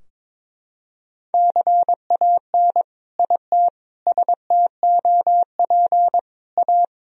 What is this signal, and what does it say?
Morse code, heard as a keyed tone, about 22 wpm, 700 Hz. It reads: CAN IT STOP A
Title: Sentences from Top 1000 Words - Review Part 2 - 22wpm